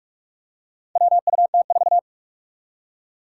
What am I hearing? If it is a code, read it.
Morse code, heard as a keyed tone, about 46 wpm, 700 Hz. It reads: WUT4
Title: Four Challenging Letter-Number Combinations 45wpm